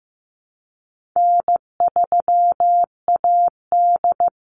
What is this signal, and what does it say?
Morse code, heard as a keyed tone, about 15 wpm, 700 Hz. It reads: N3AD
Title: Easy Call Signs - 15wpm